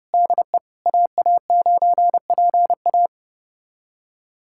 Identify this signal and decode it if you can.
Morse code, heard as a keyed tone, about 30 wpm, 700 Hz. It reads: DE AA9PA
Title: USA Callsign DE Callsign - 5 Characters - 30wpm